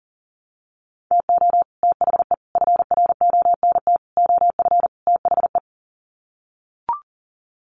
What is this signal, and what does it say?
Morse code, heard as a keyed tone, about 40 wpm, 700 Hz. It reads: TO THE FRONT OF THE
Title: Sets of 5 Words 40wpm